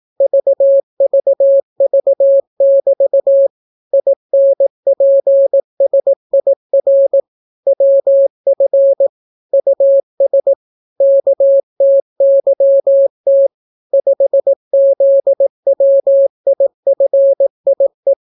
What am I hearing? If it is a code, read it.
Morse code, heard as a keyed tone, about 18 wpm, 550 Hz. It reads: VVV<BT> INPSIR WF US KTYT 5ZWIFIE